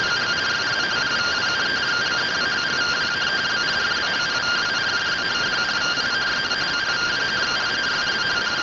Two-tone teleprinter signal, about 170 Hz shift.